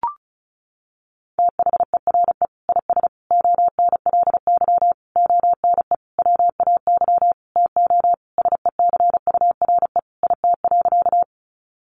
Morse code, heard as a keyed tone, about 34 wpm, 700 Hz. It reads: THERE IS ONLY ONE WAY TO SECURE IT.
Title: Sentences from Top 1200 Words 35wpm